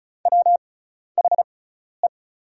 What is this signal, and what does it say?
Morse code, heard as a keyed tone, about 36 wpm, spaced as for 12 wpm, 700 Hz. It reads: WHE
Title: 6x Character Spacing - Top 200 Words - 35wpm